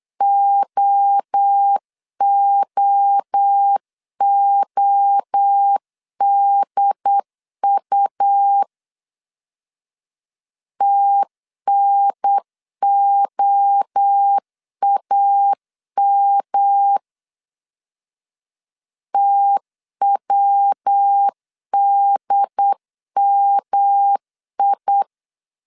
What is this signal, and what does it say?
Morse code, heard as a keyed tone, about 8 wpm, 800 Hz. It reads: OOODU TNOAM TWDMI
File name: CW / DET / 1 3 1500C S B 8wpm 4141